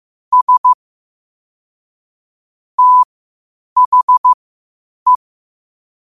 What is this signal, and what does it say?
Morse code, heard as a keyed tone, about 15 wpm, spaced as for 8 wpm, 1000 Hz. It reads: S THE
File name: qotdfiles / 8wpm